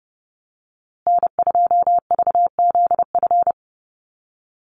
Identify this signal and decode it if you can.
Morse code, heard as a keyed tone, about 30 wpm, 700 Hz. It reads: N2VZF
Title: USA Call Signs Repeated 3x 30wpm